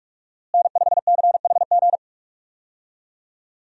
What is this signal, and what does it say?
Morse code, heard as a keyed tone, about 46 wpm, 700 Hz. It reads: N5CHG
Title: USA Call Signs - 5 Characters - Repeated 2x - 45wpm